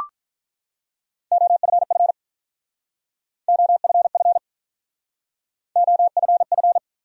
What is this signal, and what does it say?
Morse code, heard as a keyed tone, about 45 wpm, 700 Hz. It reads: OFF OFF OFF